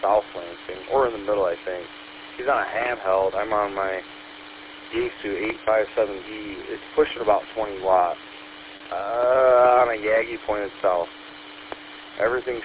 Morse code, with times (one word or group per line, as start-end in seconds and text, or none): none